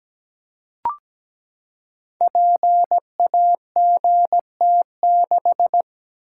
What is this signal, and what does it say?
Morse code, heard as a keyed tone, about 17 wpm, 700 Hz. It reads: PAGT6
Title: Code Groups - 5 Letter:Numbers - 17wpm